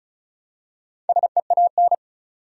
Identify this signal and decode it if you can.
Morse code, heard as a keyed tone, about 36 wpm, 700 Hz. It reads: SEAN
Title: USA Common Names - Max 4 Letters - 35wpm